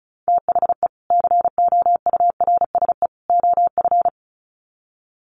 Morse code, heard as a keyed tone, about 34 wpm, 700 Hz. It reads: THE COURSE OF
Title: Sets of 3 Words 35wpm